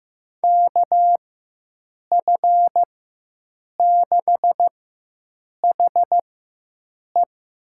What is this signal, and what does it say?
Morse code, heard as a keyed tone, about 15 wpm, spaced as for 7 wpm, 700 Hz. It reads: KF6HE
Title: USA Call Signs - 4x Character Spacing - 15wpm